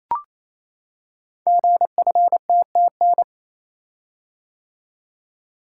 Morse code, heard as a keyed tone, about 27 wpm, 700 Hz. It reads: GFTTN
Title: Code Groups - 5 Letters - 28wpm